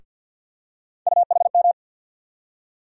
Morse code, heard as a keyed tone, about 52 wpm, 700 Hz. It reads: UHM
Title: USA Call Sign Postfixes - 3 Letters 50wpm